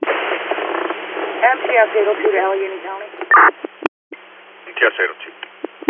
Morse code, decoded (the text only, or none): none